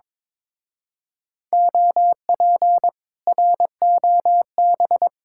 Morse code, heard as a keyed tone, about 22 wpm, 700 Hz. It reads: OP ROB
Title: Speed-Racing Common QSO Phrases - 22wpm